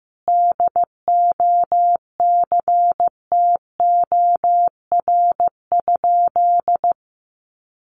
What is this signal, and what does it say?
Morse code, heard as a keyed tone, about 14 wpm, 700 Hz. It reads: DOCTOR?
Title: Sentences from Top 1500 Words 15wpm